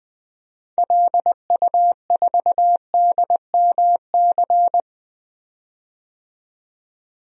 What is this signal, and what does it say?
Morse code, heard as a keyed tone, about 21 wpm, 700 Hz. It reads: LU4DMC
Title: DXCC-100 Call Signs - 6 Characters - 20wpm